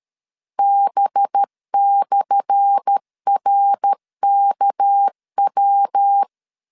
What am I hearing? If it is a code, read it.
Morse code, heard as a keyed tone, about 13 wpm, 800 Hz. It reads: B/RKW
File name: LEZIONI / 10 SIGNS / 10A SLASH 12wpm 2314